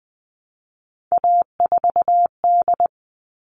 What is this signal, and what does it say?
Morse code, heard as a keyed tone, about 20 wpm, 700 Hz. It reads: A4D
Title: All Cut Numbers - 3-Digit 20wpm